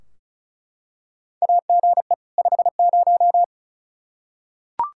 Morse code, heard as a keyed tone, about 36 wpm, 700 Hz. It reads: AGE 50
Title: K1USN SST QSO Practice 35wpm